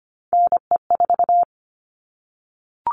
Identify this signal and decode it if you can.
Morse code, heard as a keyed tone, about 25 wpm, 700 Hz. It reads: NE4
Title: All Cut Numbers - 3-Digit 25wpm